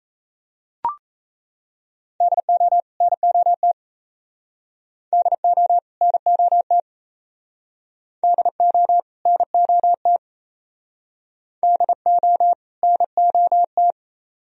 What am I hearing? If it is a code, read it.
Morse code, heard as a keyed tone, about 35 wpm, 700 Hz. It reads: DO NOT DO NOT DO NOT DO NOT
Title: Speed-Racing - Sets of 2 Words 28wpm